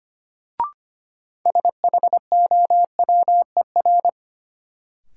Morse code, heard as a keyed tone, about 25 wpm, 700 Hz. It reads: SHOWER
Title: Sentences from Top 1600 Words - New Word Drill - 25wpm